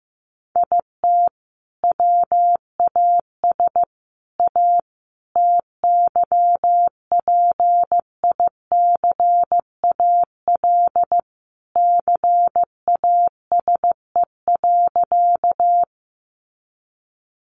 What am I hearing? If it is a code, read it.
Morse code, heard as a keyed tone, about 15 wpm, 700 Hz. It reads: IT WAS A TYPICAL CASE.
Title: Sentences from Top 1100 Words 15wpm